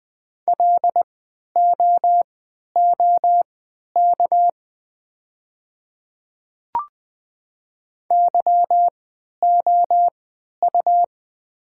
Morse code, heard as a keyed tone, about 20 wpm, spaced as for 11 wpm, 700 Hz. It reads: LOOK YOU
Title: 3x Character Spacing - Top 200 Words - 20wpm